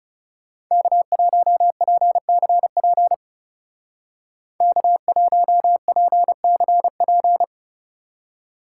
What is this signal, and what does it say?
Morse code, heard as a keyed tone, about 40 wpm, 700 Hz. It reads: K1PCP K1PCP